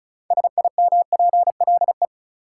Morse code, heard as a keyed tone, about 34 wpm, 700 Hz. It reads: SIMPLE
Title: Sentences from Top 400 Words - New Word Drill - 35wpm